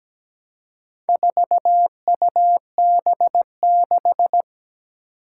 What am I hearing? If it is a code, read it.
Morse code, heard as a keyed tone, about 17 wpm, 700 Hz. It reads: 4UB6
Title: Four Challenging Letter-Number Combinations 17wpm